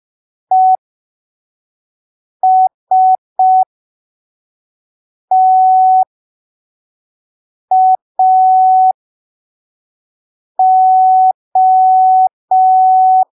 Morse code, heard as a keyed tone, about 5 wpm, spaced as for 3 wpm, 750 Hz. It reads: ESTAO
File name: arquivos / texto / texto 01 5ppm